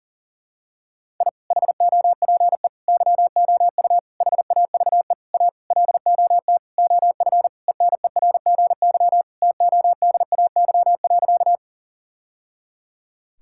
Morse code, heard as a keyed tone, about 40 wpm, 700 Hz. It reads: I HOPE YOU HAVE A LOT OF ENERGY TODAY.
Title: Sentences from Top 1000 Words - Review Part 6 - 40wpm